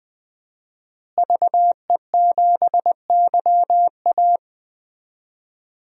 Morse code, heard as a keyed tone, about 21 wpm, 700 Hz. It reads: VE7YA
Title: DXCC-100 Call Signs - 5 Characters - Repeated 2x - 20wpm